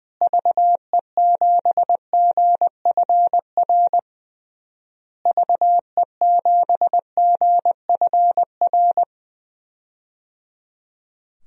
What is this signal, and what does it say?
Morse code, heard as a keyed tone, about 20 wpm, 700 Hz. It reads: VE7GFR VE7GFR
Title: DXCC-100 Call Signs - 6 Characters - Repeated 3x - 20wpm